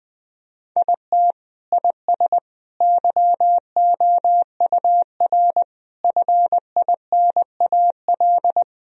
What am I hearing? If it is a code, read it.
Morse code, heard as a keyed tone, about 20 wpm, 700 Hz. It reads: IT IS YOUR FINAL
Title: Sentences from Top 1000 Words - Review Part 2 - 20wpm